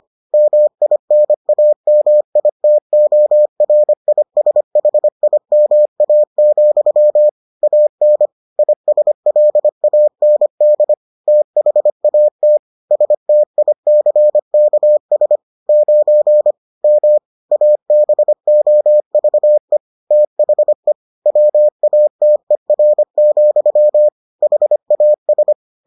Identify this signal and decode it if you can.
Morse code, heard as a keyed tone, about 25 wpm, 600 Hz. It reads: MINAMITORISHIMA, AN ISLAND THAT STICKS 9 M ABOVE THE WATER, HAS